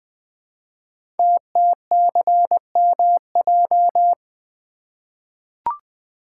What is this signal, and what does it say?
Morse code, heard as a keyed tone, about 20 wpm, 700 Hz. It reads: TTCMJ
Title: Code Groups - 5 Letters - 20wpm